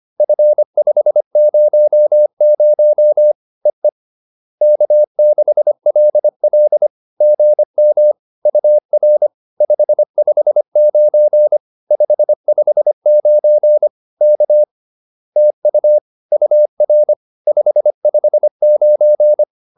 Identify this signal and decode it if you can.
Morse code, heard as a keyed tone, about 25 wpm, 600 Hz. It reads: F500 EE K6LL GM UR 559 559 K TU UR 559
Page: advanced-practice-files